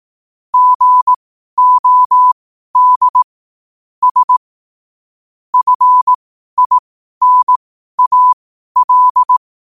Morse code, heard as a keyed tone, about 18 wpm, spaced as for 12 wpm, 1000 Hz. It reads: GOD S FINAL
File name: qotdfiles / 12wpm